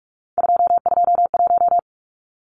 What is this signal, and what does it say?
Morse code, heard as a keyed tone, about 44 wpm, 700 Hz. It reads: 221
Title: Numbers - 3-Digit 45wpm